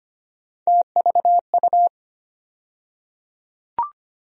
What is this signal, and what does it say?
Morse code, heard as a keyed tone, about 25 wpm, 700 Hz. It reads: TVU
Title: USA Call Sign Postfixes - 3 Letters 25wpm